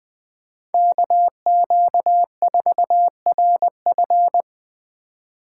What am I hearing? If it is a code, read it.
Morse code, heard as a keyed tone, about 21 wpm, 700 Hz. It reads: KQ4RF
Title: USA Call Signs - 5 Characters - 3x Character Spacing - 20wpm